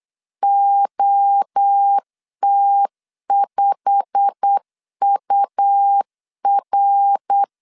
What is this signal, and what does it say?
Morse code, heard as a keyed tone, about 8 wpm, 800 Hz. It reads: OT5UR
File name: LEZIONI / 9 2 8wpm 3437